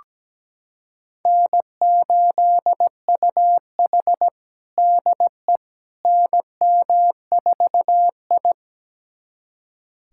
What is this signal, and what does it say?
Morse code, heard as a keyed tone, about 17 wpm, 700 Hz. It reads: N8UH DE NM4I
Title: USA Callsign DE Callsign - 4 Characters - 17wpm